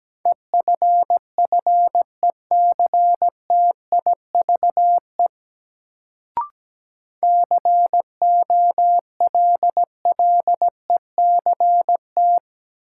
Morse code, heard as a keyed tone, about 17 wpm, 700 Hz. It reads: EFFECTIVE COLLECT
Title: Sentences from Top 1600 Words - New Word Drill Alt - 17wpm